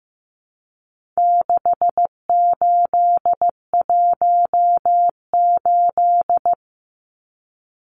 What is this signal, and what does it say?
Morse code, heard as a keyed tone, about 15 wpm, 700 Hz. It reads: 6818
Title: Numbers - 4-Digit 15wpm